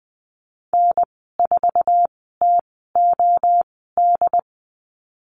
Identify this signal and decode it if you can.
Morse code, heard as a keyed tone, about 19 wpm, spaced as for 14 wpm, 700 Hz. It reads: N4TOD
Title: 076 Lesson 13g - Letter L - Call Signs 20wpm